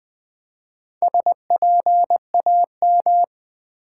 Morse code, heard as a keyed tone, about 20 wpm, 700 Hz. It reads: SPAM
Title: Build A Word - 1 Letter At A Time - 3 to 5-Letter Words - 20wpm